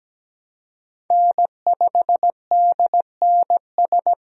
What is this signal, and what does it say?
Morse code, heard as a keyed tone, about 17 wpm, 700 Hz. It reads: N5DNS
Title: USA Call Signs - 5 Characters - 17wpm